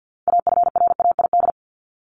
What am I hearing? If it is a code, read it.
Morse code, heard as a keyed tone, about 50 wpm, 700 Hz. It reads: AFRAID
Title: Top 1251-1500 Words 50wpm